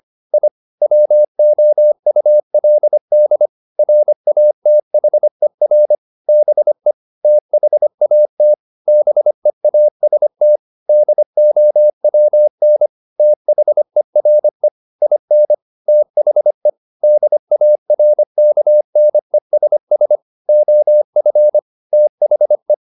Morse code, heard as a keyed tone, about 25 wpm, 600 Hz. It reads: I WOULD RATHER BE THAT BEAST DOWN THERE IN THE DARKNESS OF THE
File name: the old man and the sea part 1 25wpm